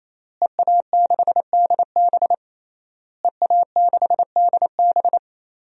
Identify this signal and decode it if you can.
Morse code, heard as a keyed tone, about 28 wpm, 700 Hz. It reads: EA6DB EA6DB
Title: DXCC-100 Call Signs - 5 Characters - Repeated 3x - 28wpm